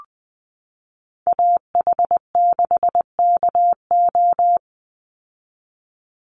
Morse code, heard as a keyed tone, about 20 wpm, 700 Hz. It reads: AH6KO
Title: Call Signs Repeated 3x - 20wpm